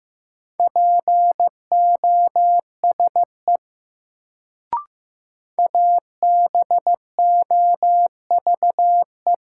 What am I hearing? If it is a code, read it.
Morse code, heard as a keyed tone, about 15 wpm, 700 Hz. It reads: POSE ABOVE
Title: Sentences from Top 1000 Words - Review Part 2 - New Word Drill - 15wpm